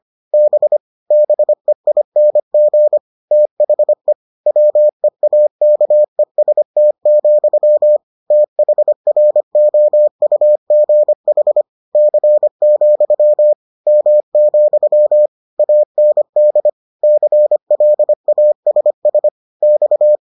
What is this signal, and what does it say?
Morse code, heard as a keyed tone, about 25 wpm, 600 Hz. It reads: B BEING THE WEAKEST, THROUGH C, M, AND CLASS X